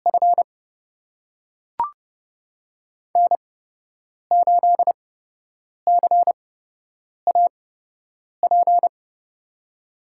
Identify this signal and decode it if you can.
Morse code, heard as a keyed tone, about 30 wpm, spaced as for 8 wpm, 700 Hz. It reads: F N8CAP
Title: USA Call Signs - 5 Characters - 8x Character Spacing - 30wpm